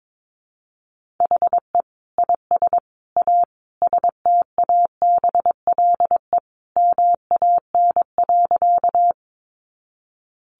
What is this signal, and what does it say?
Morse code, heard as a keyed tone, about 22 wpm, 700 Hz. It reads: HE IS A STABLE MAN.
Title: Sentences from Top 1500 Words 22wpm